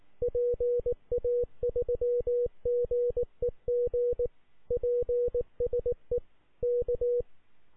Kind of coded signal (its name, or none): Morse code